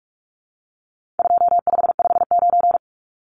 Morse code, heard as a keyed tone, about 44 wpm, 700 Hz. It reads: 2559
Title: Numbers - 4-Digit 45wpm